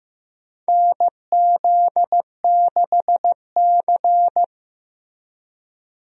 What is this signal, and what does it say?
Morse code, heard as a keyed tone, about 15 wpm, 700 Hz. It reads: NZ6C
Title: USA Call Signs - 4 Characters - Repeated 2x - 15wpm